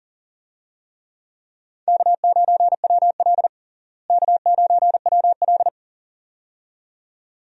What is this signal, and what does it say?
Morse code, heard as a keyed tone, about 40 wpm, 700 Hz. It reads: K9WL K9WL
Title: USA Call Signs - 4 Characters - Repeated 2x - 40wpm